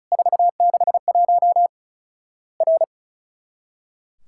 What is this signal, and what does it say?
Morse code, heard as a keyed tone, about 36 wpm, 700 Hz. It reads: 461 R
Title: Worked All Europe DX Contest - QTC Exchange - 35wpm